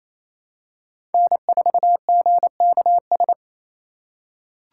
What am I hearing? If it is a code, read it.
Morse code, heard as a keyed tone, about 28 wpm, 700 Hz. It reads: N4GKS
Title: USA Call Signs - 5 Characters - 4x Character Spacing - 28wpm